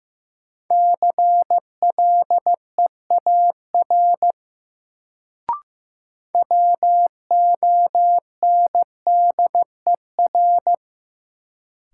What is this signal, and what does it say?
Morse code, heard as a keyed tone, about 15 wpm, 700 Hz. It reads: CLEAR WONDER
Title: Sentences from Top 500 Words - New Word Drill - 15wpm